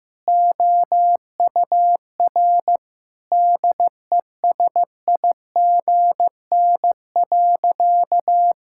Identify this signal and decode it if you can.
Morse code, heard as a keyed tone, about 15 wpm, 700 Hz. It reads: OUR DESIGN.
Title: Sentences from Top 1000 Words - Review Part 5 - 15wpm